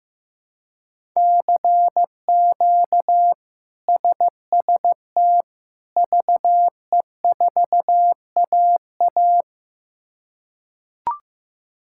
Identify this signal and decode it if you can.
Morse code, heard as a keyed tone, about 16 wpm, 700 Hz. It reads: CQ SST VE4AA
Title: K1USN SST QSO Practice - 15wpm